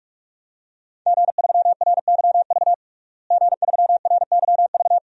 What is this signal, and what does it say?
Morse code, heard as a keyed tone, about 42 wpm, 700 Hz. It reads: G3RYV G3RYV
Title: DXCC-100 Call Signs - 5 Characters - Repeated 2x - 45wpm